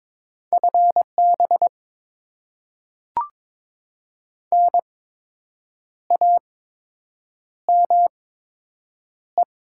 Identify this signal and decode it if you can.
Morse code, heard as a keyed tone, about 22 wpm, spaced as for 7 wpm, 700 Hz. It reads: FB NAME
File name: Top 20 QSO Elements - 8x Character Spacing - 22wpm